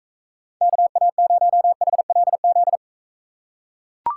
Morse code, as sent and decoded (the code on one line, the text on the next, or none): -.- .- ----- .... .-.. --..
KA0HLZ